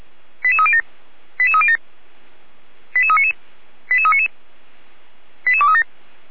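Five-tone selective call (ZVEI), heard as quick runs of five tones, tones of about 70 ms. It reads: 88307 88307 88399 88399 88237
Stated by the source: ZVEI five-tone selective call received on SDR, Feuerwehr Orlamünde.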